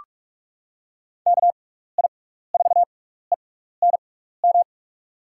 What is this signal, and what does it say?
Morse code, heard as a keyed tone, about 44 wpm, spaced as for 15 wpm, 700 Hz. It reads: KI4ENM
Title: USA Call Signs - 6x Character Spacing - 45wpm